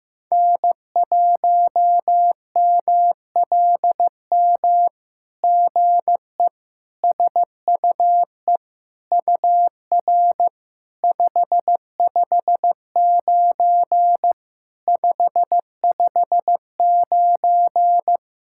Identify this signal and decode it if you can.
Morse code, heard as a keyed tone, about 16 wpm, 700 Hz. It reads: N1MLM GE SUE UR 559 559
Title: SOTA QSO Practice 15wpm